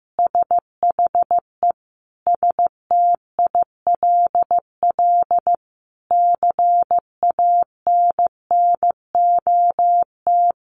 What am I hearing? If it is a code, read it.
Morse code, heard as a keyed tone, about 15 wpm, 700 Hz. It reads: SHE STILL CANNOT
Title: Additional Easy Phrases 15wpm